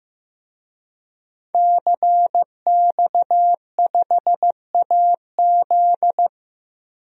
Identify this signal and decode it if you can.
Morse code, heard as a keyed tone, about 15 wpm, 700 Hz. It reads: CX5AZ
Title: DXCC-100 Call Signs - 5 Characters - Repeated 3x - 15wpm